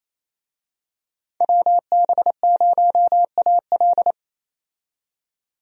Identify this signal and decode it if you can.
Morse code, heard as a keyed tone, about 28 wpm, 700 Hz. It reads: WB0AL
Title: USA Call Signs - 5 Characters - 28wpm